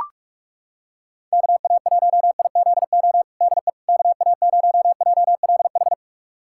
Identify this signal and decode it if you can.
Morse code, heard as a keyed tone, about 42 wpm, 700 Hz. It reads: KA1IZO DE KA0JLH
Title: USA Callsign DE Callsign - 6 Characters - 45wpm